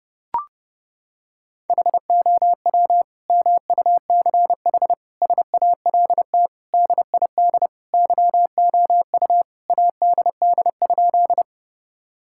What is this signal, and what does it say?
Morse code, heard as a keyed tone, about 32 wpm, 700 Hz. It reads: HOW MUCH SALT DID YOU ADD?
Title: Sentences from Top 1000 Words - Review Part 5 - 30wpm